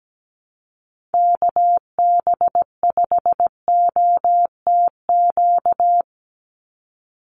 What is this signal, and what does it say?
Morse code, heard as a keyed tone, about 17 wpm, 700 Hz. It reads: KB5OTQ
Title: USA Call Signs 17wpm